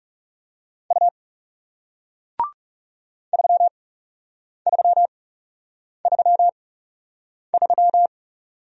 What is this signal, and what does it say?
Morse code, heard as a keyed tone, about 40 wpm, spaced as for 8 wpm, 700 Hz. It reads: U 3333